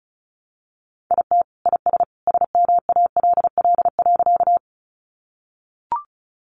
Morse code, heard as a keyed tone, about 34 wpm, 700 Hz. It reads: IT IS SMALL.
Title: Additional Easy Phrases 35wpm